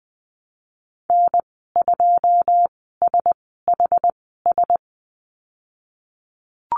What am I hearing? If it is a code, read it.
Morse code, heard as a keyed tone, about 20 wpm, spaced as for 15 wpm, 700 Hz. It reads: N2SHS
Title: 082 Lesson 14g - Number 2 - Call Signs 20wpm